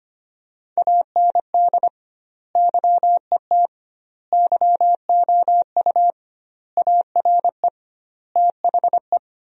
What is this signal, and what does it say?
Morse code, heard as a keyed tone, about 25 wpm, 700 Hz. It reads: AND YET YOU ARE THE
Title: 2x Word-Spacing - Sentences from Top 500 Words - 25wpm